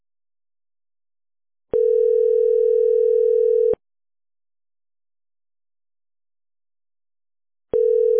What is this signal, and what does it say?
Ringback tone (North America), 440 + 480 Hz.